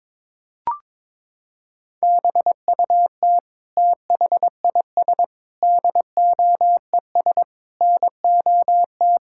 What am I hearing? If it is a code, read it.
Morse code, heard as a keyed tone, about 22 wpm, 700 Hz. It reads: BUT THIS DOES NOT
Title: Sentences from Top 1000 Words - Review Part 6 - 22wpm